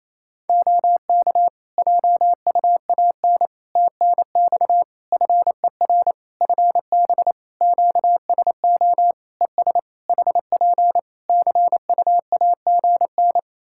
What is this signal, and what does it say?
Morse code, heard as a keyed tone, about 28 wpm, 700 Hz. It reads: OK JUAN TNX FER FB QSO ES HP CUAGN
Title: Practice QSO - Exchange 4 - Ending - 28wpm